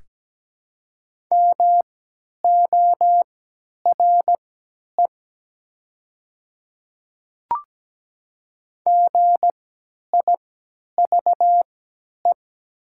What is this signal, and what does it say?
Morse code, heard as a keyed tone, about 17 wpm, spaced as for 10 wpm, 700 Hz. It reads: MORE GIVE
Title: 3x Character Spacing - Top 200 Words - 17wpm